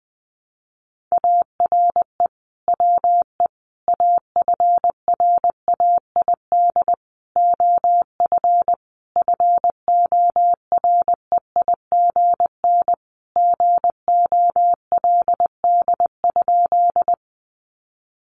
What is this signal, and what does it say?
Morse code, heard as a keyed tone, about 20 wpm, 700 Hz. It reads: ARE WE AFRAID OF FOREIGN GOLD?
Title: Sentences from Top 1500 Words 20wpm